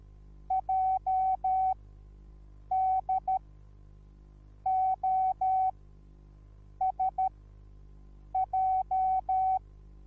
Morse code, heard as a keyed tone, about 13 wpm, spaced as for 6 wpm, 760 Hz. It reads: JDOSJ